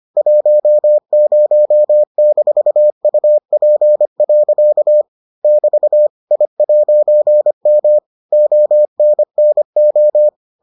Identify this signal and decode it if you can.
Morse code, heard as a keyed tone, about 25 wpm, 600 Hz. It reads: 10-UP. <BT> I'M ONNO